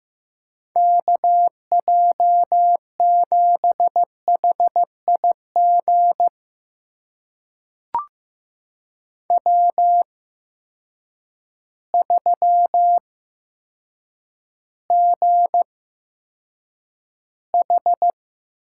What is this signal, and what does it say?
Morse code, heard as a keyed tone, about 15 wpm, spaced as for 8 wpm, 700 Hz. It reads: KJ7HIG W3GH
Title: USA Call Signs - 8x Character Spacing - 15wpm